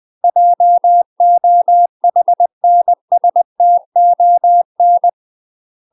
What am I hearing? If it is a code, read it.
Morse code, heard as a keyed tone, about 20 wpm, 700 Hz. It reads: JOHNSTON
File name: UK MASTER QSOs TRAINING/20WPM